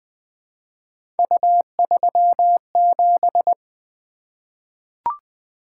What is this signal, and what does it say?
Morse code, heard as a keyed tone, about 20 wpm, 700 Hz. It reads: U37
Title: Three Letter-Number Combinations - 20wpm